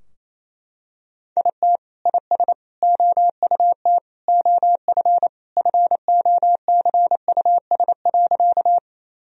Morse code, heard as a keyed tone, about 28 wpm, 700 Hz. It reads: IT IS OUT OF FOCUS.
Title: Sentences from Top 1000 Words - Review Part 4 - 28wpm